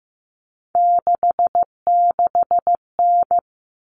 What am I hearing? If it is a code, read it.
Morse code, heard as a keyed tone, about 15 wpm, 700 Hz. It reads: 66N
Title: All Cut Numbers - 3-Digit 15wpm